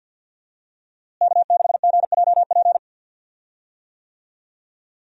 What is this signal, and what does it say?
Morse code, heard as a keyed tone, about 52 wpm, 700 Hz. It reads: K6GJP
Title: USA Call Signs - 5 Characters - Repeated 2x - 50wpm